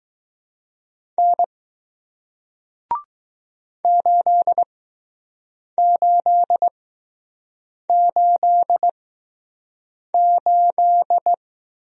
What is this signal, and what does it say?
Morse code, heard as a keyed tone, about 20 wpm, spaced as for 6 wpm, 700 Hz. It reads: N 8888